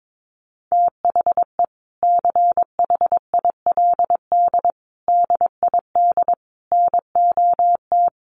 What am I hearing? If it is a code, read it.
Morse code, heard as a keyed tone, about 22 wpm, 700 Hz. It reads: THE CHILD DID NOT
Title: Sentences from Top 800 Words 22wpm